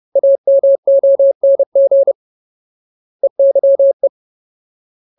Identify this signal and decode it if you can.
Morse code, heard as a keyed tone, about 30 wpm, 550 Hz. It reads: AMONG EYE
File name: Ultimate Word List / 30WPM Ultimate